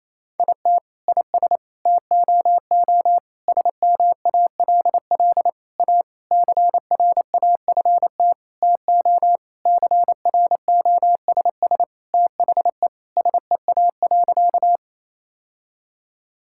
Morse code, heard as a keyed tone, about 28 wpm, 700 Hz. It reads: IT IS TOO SMALL A CRAFT TO CROSS THE SEA.
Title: Sentences from Top 1000 Words - Review Part 6 - 28wpm